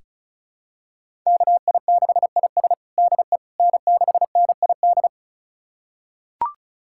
Morse code, heard as a keyed tone, about 34 wpm, 690 Hz. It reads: KI6IS DE N6NID
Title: USA Callsign DE Callsign - 5 Characters - 35wpm